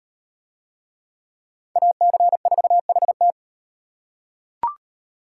Morse code, heard as a keyed tone, about 36 wpm, 700 Hz. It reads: AC4HT